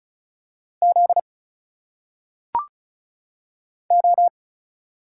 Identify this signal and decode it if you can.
Morse code, heard as a keyed tone, about 34 wpm, 700 Hz. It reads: Z O
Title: Single Letters - Phonetic Pronunciation 35wpm